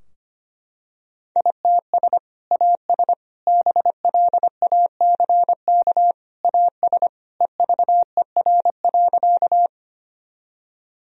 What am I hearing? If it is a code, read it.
Morse code, heard as a keyed tone, about 25 wpm, 700 Hz. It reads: ITS AS BLACK AS EVER.
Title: Sentences from Top 1000 Words - Review Part 2 - 25wpm